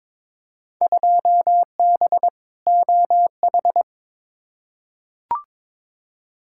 Morse code, heard as a keyed tone, about 22 wpm, 700 Hz. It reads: 2B OH
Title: Field Day Exchanges - 22wpm